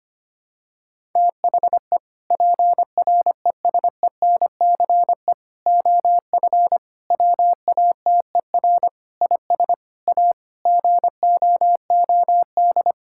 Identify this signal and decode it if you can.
Morse code, heard as a keyed tone, about 25 wpm, 700 Hz. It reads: THE PRESENCE OF WATER IS A GOOD
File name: practice / 25wpm / Sentences with 3-Word Phrases - 25wpm